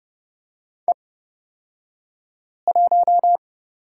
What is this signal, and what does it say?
Morse code, heard as a keyed tone, about 29 wpm, spaced as for 5 wpm, 690 Hz. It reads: E1
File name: Lesson 36d - BT - Two Characters 30wpm